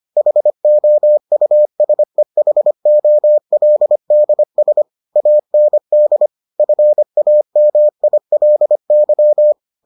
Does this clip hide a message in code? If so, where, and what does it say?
Yes: Morse code from 0.2 to 9.5 s, heard as a keyed tone, about 25 wpm, 600 Hz. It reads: HOUSEHOLDS AND FAMILY